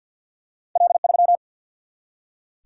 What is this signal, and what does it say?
Morse code, heard as a keyed tone, about 47 wpm, 700 Hz. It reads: L 3
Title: Single-Letter Single-Number 50wpm